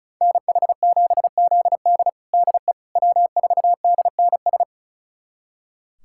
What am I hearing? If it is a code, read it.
Morse code, heard as a keyed tone, about 36 wpm, 700 Hz. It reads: NH7ZD DE W4DNS
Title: USA Callsign DE Callsign - 5 Characters - 35wpm